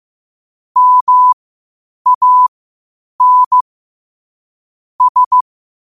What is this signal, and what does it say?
Morse code, heard as a keyed tone, about 15 wpm, spaced as for 8 wpm, 1000 Hz. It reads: MAN S
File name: qotdfiles / 8wpm